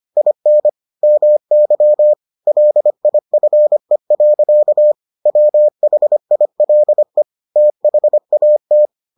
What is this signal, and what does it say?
Morse code, heard as a keyed tone, about 25 wpm, 600 Hz. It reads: IN MY LIFE. WHILE THAT